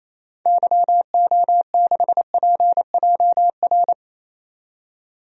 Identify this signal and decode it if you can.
Morse code, heard as a keyed tone, about 28 wpm, 700 Hz. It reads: YO6PJR
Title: DXCC-100 Call Signs - 6 Characters - 28wpm